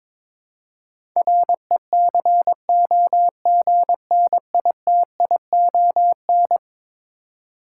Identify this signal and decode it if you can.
Morse code, heard as a keyed tone, about 22 wpm, 700 Hz. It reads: RECOGNITION
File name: practice / 22wpm / Eleven Letter Words - 22wpm